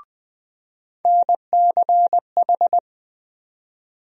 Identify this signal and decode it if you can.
Morse code, heard as a keyed tone, about 20 wpm, 700 Hz. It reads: NCH
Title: USA Call Sign Postfixes - 3 Letters 20wpm